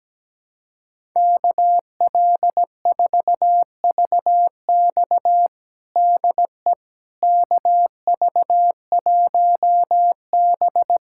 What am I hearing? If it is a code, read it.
Morse code, heard as a keyed tone, about 17 wpm, 700 Hz. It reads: KL4VX DE KV1B
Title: USA Callsign DE Callsign 17wpm